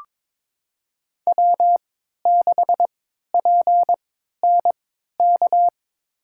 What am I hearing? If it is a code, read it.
Morse code, heard as a keyed tone, about 22 wpm, spaced as for 13 wpm, 700 Hz. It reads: W6PNK
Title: USA Call Signs - 3x Character Spacing - 22wpm